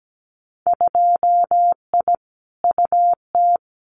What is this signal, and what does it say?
Morse code, heard as a keyed tone, about 17 wpm, 700 Hz. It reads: 2I UT
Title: Winter Field Day Exchanges 17wpm